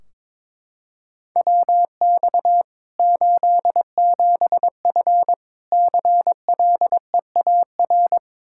Morse code, heard as a keyed tone, about 22 wpm, 700 Hz. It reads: WX 87F CLEAR
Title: Practice QSO - Exchange 2 - Info - 22wpm